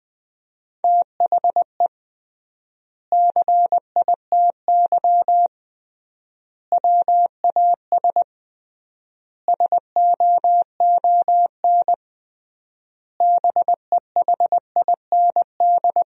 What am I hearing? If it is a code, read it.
Morse code, heard as a keyed tone, about 21 wpm, 700 Hz. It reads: THE CITY WAS SOON BEHIND
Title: 3x Word-Spacing - Sentences from Top 500 Words - 20wpm